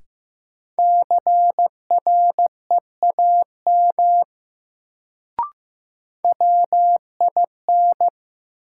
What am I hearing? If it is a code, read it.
Morse code, heard as a keyed tone, about 15 wpm, 700 Hz. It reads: CREAM WIN
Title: Sentences from Top 1500 Words - New Word Drill - 15wpm